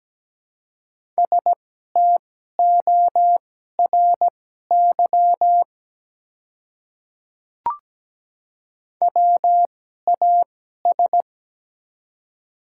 Morse code, heard as a keyed tone, about 17 wpm, spaced as for 12 wpm, 700 Hz. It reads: STORY WAS
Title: 2x Character Spacing - Top 300 Words - 17wpm